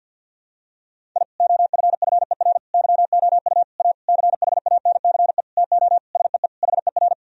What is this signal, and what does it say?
Morse code, heard as a keyed tone, about 52 wpm, 700 Hz. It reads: I OFFER YOU A CHANCE TO SEE HER
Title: Sentences from Top 1000 Words - Review Part 4 - 50wpm